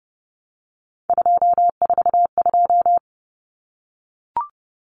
Morse code, heard as a keyed tone, about 30 wpm, 700 Hz. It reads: 242
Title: Numbers - 3-Digit 30wpm